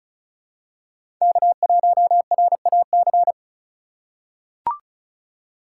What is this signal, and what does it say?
Morse code, heard as a keyed tone, about 34 wpm, 700 Hz. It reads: K1RAC
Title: USA Call Signs - 5 Characters - 4x Character Spacing - 35wpm